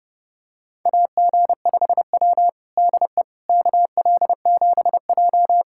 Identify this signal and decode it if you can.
Morse code, heard as a keyed tone, about 32 wpm, 700 Hz. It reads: AG5W DE KL7J
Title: USA Callsign DE Callsign - 4 Characters - 30wpm